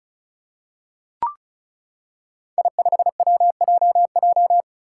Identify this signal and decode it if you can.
Morse code, heard as a keyed tone, about 36 wpm, 700 Hz. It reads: I5WJJ
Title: Five Challenging Letter-Number Combinations 35wpm